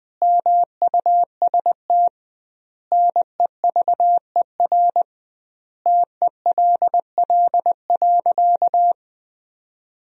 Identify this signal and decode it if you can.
Morse code, heard as a keyed tone, about 20 wpm, 700 Hz. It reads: MUST NEVER TELL.
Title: 2x Word-Spacing - Sentences from Top 300 Words - 20wpm